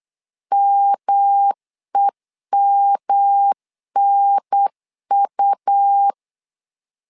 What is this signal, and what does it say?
Morse code, heard as a keyed tone, about 8 wpm, 800 Hz. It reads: MEMNU
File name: LEZIONI / 3 MWRO / 3 M 8wpm 2300